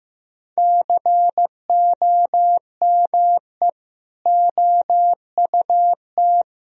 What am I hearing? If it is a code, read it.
Morse code, heard as a keyed tone, about 16 wpm, 700 Hz. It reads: COME OUT
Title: Sentences from Top 1000 Words - Review Part 2 - 15wpm